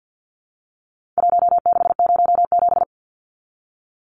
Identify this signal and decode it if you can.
Morse code, heard as a keyed tone, about 49 wpm, 700 Hz. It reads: 1607
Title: Numbers - Contest Style - 4-Digit 50wpm